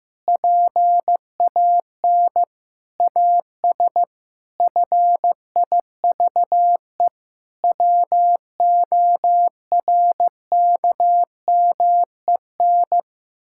Morse code, heard as a keyed tone, about 15 wpm, 700 Hz. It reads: PAN AS FIVE WORKMEN
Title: Pangram Challenge v2 - 15wpm